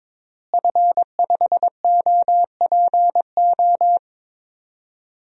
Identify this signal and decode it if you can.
Morse code, heard as a keyed tone, about 22 wpm, 700 Hz. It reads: F5OPO
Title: DXCC-100 Call Signs - 5 Characters - Repeated 3x - 22wpm